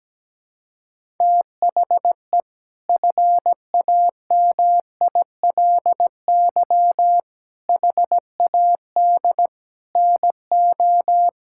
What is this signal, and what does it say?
Morse code, heard as a keyed tone, about 17 wpm, 700 Hz. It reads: THE FAMILY HAD NO
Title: Sentences from Top 1600 Words 17wpm